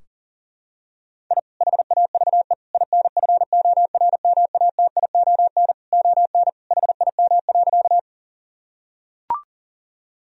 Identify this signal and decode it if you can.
Morse code, heard as a keyed tone, about 40 wpm, 700 Hz. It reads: I HAVE INFORMATION ON HIM.
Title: Sentences from Top 1000 Words - Review Part 3 - 40wpm